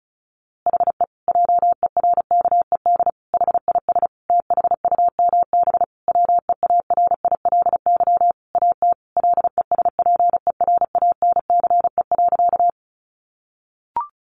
Morse code, heard as a keyed tone, about 34 wpm, 700 Hz. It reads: HE JERKED HIS THUMB WEARILY AT LESPERANCE.
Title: A Sound Of Thunder 35wpm